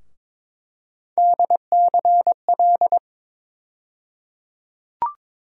Morse code, heard as a keyed tone, about 22 wpm, 700 Hz. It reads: DCL
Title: USA Call Sign Postfixes - 3 Letters 22wpm